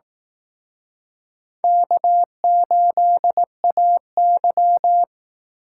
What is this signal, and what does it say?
Morse code, heard as a keyed tone, about 18 wpm, 700 Hz. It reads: K8AY
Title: Speed Racing - USA 1x2 Call Signs 15wpm